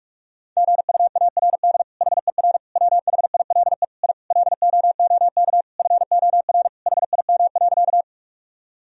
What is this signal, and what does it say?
Morse code, heard as a keyed tone, about 46 wpm, 700 Hz. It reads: GUARD HER WHILE I LOOK FOR HIM.
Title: Sentences from Top 1000 Words - Review Part 4 - 45wpm